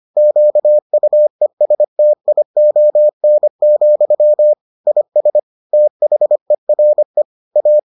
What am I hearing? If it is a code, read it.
Morse code, heard as a keyed tone, about 25 wpm, 600 Hz. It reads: QUESTION, IS THERE A